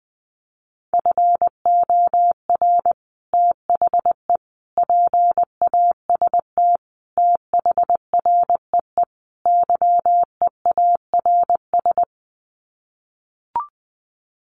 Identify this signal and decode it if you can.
Morse code, heard as a keyed tone, about 20 wpm, 700 Hz. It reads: FOR THE PAST THREE YEARS
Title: Sets of 5 Words Encore2 20wpm